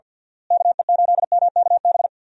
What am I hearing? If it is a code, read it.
Morse code, heard as a keyed tone, about 47 wpm, 700 Hz. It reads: KE8MKB
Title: USA Call Signs - 6 Characters - Repeated 2x - 50wpm